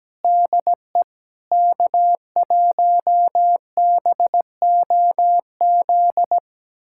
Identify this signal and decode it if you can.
Morse code, heard as a keyed tone, about 17 wpm, 700 Hz. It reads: DE K1BOZ
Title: USA Callsign DE Callsign - 5 Characters - 17wpm